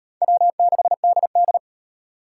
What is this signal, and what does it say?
Morse code, heard as a keyed tone, about 31 wpm, 700 Hz. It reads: W6DD